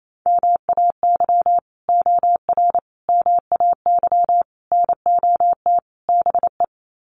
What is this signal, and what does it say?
Morse code, heard as a keyed tone, about 28 wpm, 700 Hz. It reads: MAY OR MAY NOT BE
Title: Sets of 5 Words 28wpm